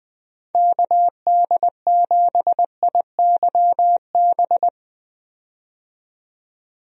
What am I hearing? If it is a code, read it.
Morse code, heard as a keyed tone, about 21 wpm, 700 Hz. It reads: KD7IYB
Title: USA Call Signs - 6 Characters - 20wpm